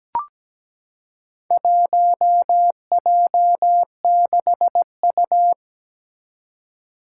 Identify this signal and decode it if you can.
Morse code, heard as a keyed tone, about 17 wpm, 700 Hz. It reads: 1J6U
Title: Four Challenging Letter-Number Combinations 17wpm